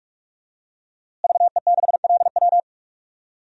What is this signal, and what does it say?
Morse code, heard as a keyed tone, about 46 wpm, 700 Hz. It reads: VE6LW
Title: DXCC-100 Call Signs - 5 Characters - Repeated 3x - 45wpm